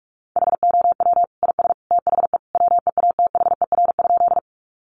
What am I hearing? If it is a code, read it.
Morse code, heard as a keyed tone, about 44 wpm, 700 Hz. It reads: HOW IS THE WEATHER?
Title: Sentences from Top 800 Words 45wpm